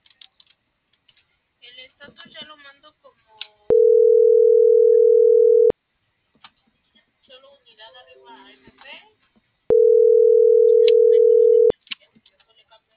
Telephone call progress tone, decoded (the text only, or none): ringback tone